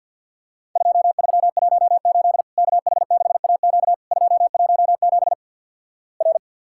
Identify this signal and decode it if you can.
Morse code, heard as a keyed tone, about 50 wpm, 700 Hz. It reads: 2318 KH6AQ 217 R
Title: Worked All Europe DX Contest - QTC Exchange - 50wpm